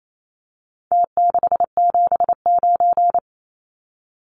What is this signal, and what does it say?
Morse code, heard as a keyed tone, about 28 wpm, 700 Hz. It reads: T679
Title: Numbers - Contest Style - 4-Digit 28wpm